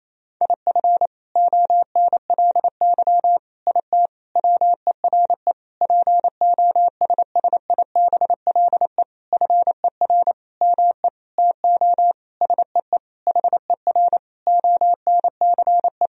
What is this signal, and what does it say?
Morse code, heard as a keyed tone, about 28 wpm, 700 Hz. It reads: IF ONLY IT WERE POSSIBLE FER ME TO SEE HER ONCE
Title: Sentences from Top 500 Words and Abbreviations 28wpm